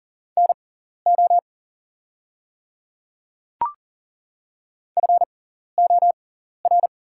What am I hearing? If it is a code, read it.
Morse code, heard as a keyed tone, about 40 wpm, spaced as for 14 wpm, 690 Hz. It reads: NO FOR
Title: 6x Character Spacing - Top 100 Words - 40wpm